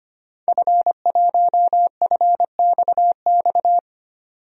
Morse code, heard as a keyed tone, about 25 wpm, 700 Hz. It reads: F1FXX
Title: DXCC-100 Call Signs - 5 Characters - 25wpm